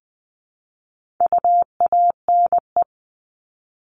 Morse code, heard as a keyed tone, about 20 wpm, 700 Hz. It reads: UANE
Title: All Cut Numbers - 4-Digit 20wpm